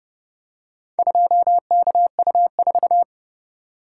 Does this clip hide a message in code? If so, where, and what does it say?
Yes: Morse code from 1.0 to 3.0 s, heard as a keyed tone, about 29 wpm, 700 Hz. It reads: 2KU4